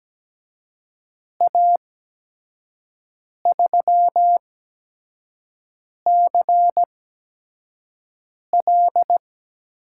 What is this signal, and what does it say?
Morse code, heard as a keyed tone, about 17 wpm, spaced as for 5 wpm, 700 Hz. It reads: A3CL
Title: USA Call Signs - 8x Character Spacing - 17wpm